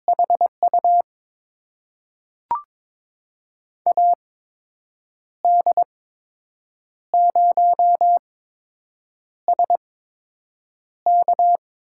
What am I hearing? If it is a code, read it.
Morse code, heard as a keyed tone, about 22 wpm, spaced as for 7 wpm, 700 Hz. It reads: HU AD0SK
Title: USA Call Signs - 5 Characters - 8x Character Spacing - 22wpm